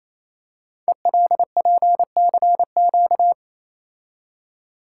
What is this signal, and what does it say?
Morse code, heard as a keyed tone, about 29 wpm, 700 Hz. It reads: ELPCQ
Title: Code Groups - 5 Letters - 28wpm